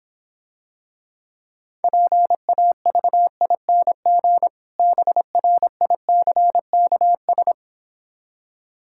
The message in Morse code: PAVING BRICKS

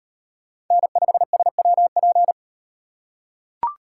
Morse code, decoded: N5SWP

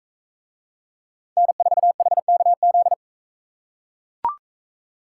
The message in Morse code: N4HKZ

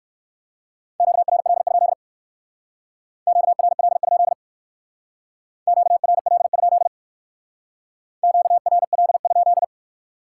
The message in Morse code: QRL? QRL? QRL? QRL?

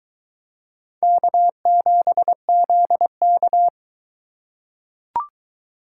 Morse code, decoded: K7ZK

